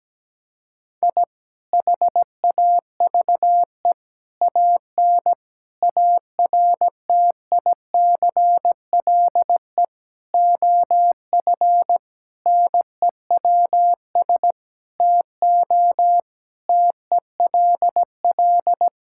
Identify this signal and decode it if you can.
Morse code, heard as a keyed tone, about 17 wpm, 700 Hz. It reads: I HAVE AN ARTICLE OF NEWS TO TELL
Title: Sentences from Top 1000 Words - Review Part 5 - 17wpm